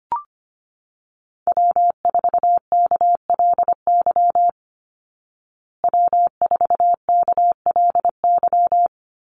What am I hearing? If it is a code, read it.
Morse code, heard as a keyed tone, about 25 wpm, 700 Hz. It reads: W4KLY W4KLY
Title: Call Signs Repeated 2x - 25wpm